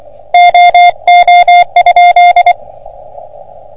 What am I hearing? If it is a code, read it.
Morse code, heard as a keyed tone, about 25 wpm, 710 Hz. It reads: OO?
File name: OM3PA PILUP CUC 388